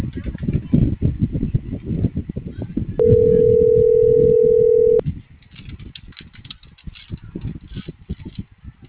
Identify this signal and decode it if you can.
Ringback tone (North America), 440 + 480 Hz.